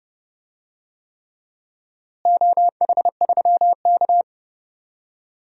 Morse code, heard as a keyed tone, about 30 wpm, 700 Hz. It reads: OH3K
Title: DXCC-100 Call Signs - 4 Characters - Repeated 3x - 30wpm